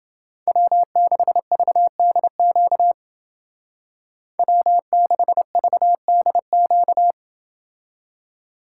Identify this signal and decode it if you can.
Morse code, heard as a keyed tone, about 28 wpm, 700 Hz. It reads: W6VDQ W6VDQ